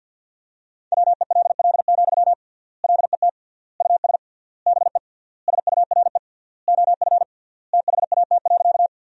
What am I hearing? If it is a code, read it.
Morse code, heard as a keyed tone, about 50 wpm, 700 Hz. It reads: WELL, LET US BE SURE OF THAT.
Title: 3x Word-Spacing - Sentences from Top 300 Words - 50wpm